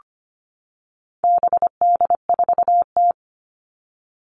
Morse code, heard as a keyed tone, about 25 wpm, 700 Hz. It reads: BD4T
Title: All Cut Numbers - 4-Digit 25wpm